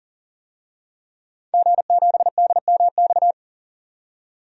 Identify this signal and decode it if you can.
Morse code, heard as a keyed tone, about 37 wpm, 700 Hz. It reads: G7DMX
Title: DXCC-100 Call Signs - 5 Characters - 40wpm